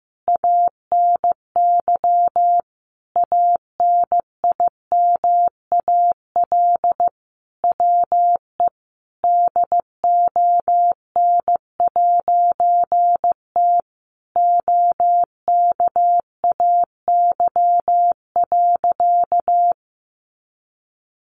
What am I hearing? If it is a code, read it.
Morse code, heard as a keyed tone, about 14 wpm, 700 Hz. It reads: ANY ANIMAL WE DON'T OKAY.
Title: A Sound Of Thunder 15wpm